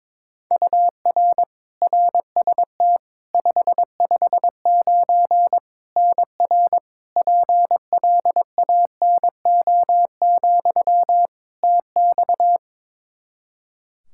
Morse code, heard as a keyed tone, about 21 wpm, 700 Hz. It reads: UR RST 559 NR PLANO, TX
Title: Practice QSO - Exchange 1 - Beginning - 22wpm